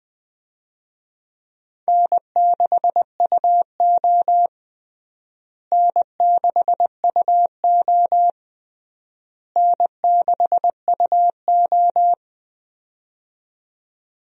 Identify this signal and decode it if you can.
Morse code, heard as a keyed tone, about 20 wpm, 700 Hz. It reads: N6UO N6UO N6UO
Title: USA Call Signs - 4 Characters - Repeated 3x - 20wpm